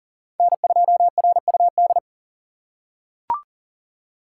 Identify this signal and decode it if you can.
Morse code, heard as a keyed tone, about 40 wpm, 700 Hz. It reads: N2RUD
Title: USA Call Signs - 2x Character Spacing - 40wpm